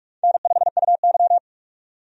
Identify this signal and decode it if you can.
Morse code, heard as a keyed tone, about 41 wpm, 700 Hz. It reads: N5UY
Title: USA Call Signs - 4 Characters - 45wpm